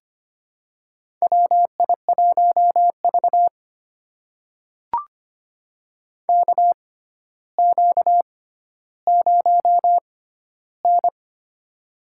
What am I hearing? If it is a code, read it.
Morse code, heard as a keyed tone, about 25 wpm, spaced as for 13 wpm, 700 Hz. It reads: WI1V KQ0N
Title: USA Call Signs - 6x Character Spacing - 25wpm